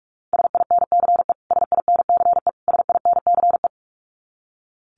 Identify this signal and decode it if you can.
Morse code, heard as a keyed tone, about 44 wpm, 700 Hz. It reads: SINCE SINCE SINCE
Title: Top 500 Words Repeated 45wpm